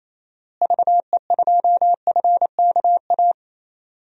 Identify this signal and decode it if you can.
Morse code, heard as a keyed tone, about 28 wpm, 700 Hz. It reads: VE2FKA
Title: DXCC-100 Call Signs - 6 Characters - Repeated 3x - 28wpm